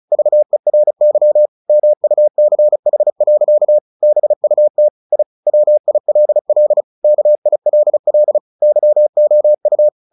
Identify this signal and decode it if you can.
Morse code, heard as a keyed tone, about 34 wpm, 600 Hz. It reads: VERY MUCH. BUT I WILL KILL YOU
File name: the old man and the sea / the old man and the sea 1 35wpm